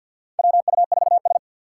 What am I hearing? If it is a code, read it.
Morse code, heard as a keyed tone, about 52 wpm, 690 Hz. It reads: WU4S
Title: USA Call Signs - 6x Character Spacing - 50wpm